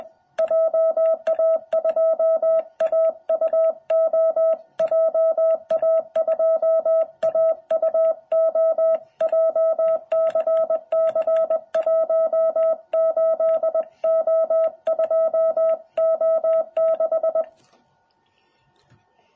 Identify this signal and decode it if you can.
Morse code, heard as a keyed tone, about 21 wpm, 660 Hz. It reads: JA2AUO JA2AUO JCC18O2O6
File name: jh8cbh / 2021cw